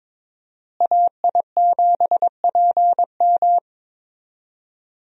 Morse code, heard as a keyed tone, about 22 wpm, 700 Hz. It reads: AI7PM
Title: USA Call Signs - 5 Characters - Repeated 3x - 22wpm